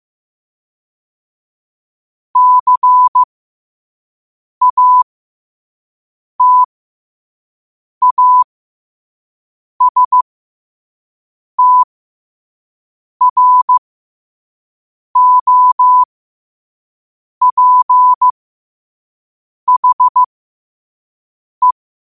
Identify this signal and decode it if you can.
Morse code, heard as a keyed tone, about 15 wpm, spaced as for 5 wpm, 1000 Hz. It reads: CATASTROPHE
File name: qotdfiles / 5wpm